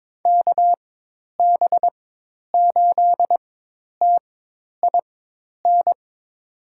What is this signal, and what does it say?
Morse code, heard as a keyed tone, about 22 wpm, spaced as for 10 wpm, 700 Hz. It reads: KB8TIN
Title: USA Call Signs - 6 Characters - 4x Character Spacing - 22wpm